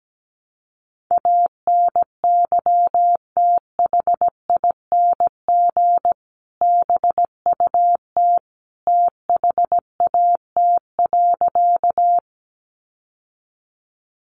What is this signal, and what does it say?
Morse code, heard as a keyed tone, about 17 wpm, 700 Hz. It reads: ANYTHING BUT THAT.
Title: Sentences from Top 1000 Words 17wpm